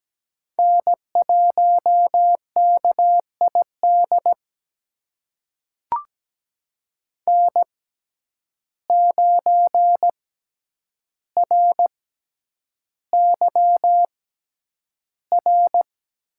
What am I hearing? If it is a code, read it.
Morse code, heard as a keyed tone, about 17 wpm, spaced as for 9 wpm, 700 Hz. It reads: N1KID N9RYR
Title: USA Call Signs - 6x Character Spacing - 17wpm